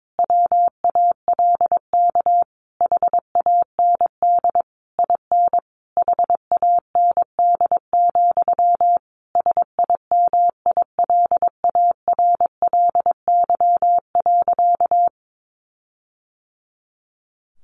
Morse code, heard as a keyed tone, about 22 wpm, 700 Hz. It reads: WALK HAND IN HAND, SIMILARLY.
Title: Sentences from Top 1400 Words 22wpm